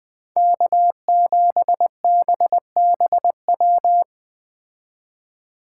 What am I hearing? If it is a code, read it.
Morse code, heard as a keyed tone, about 20 wpm, 700 Hz. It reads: K7BBW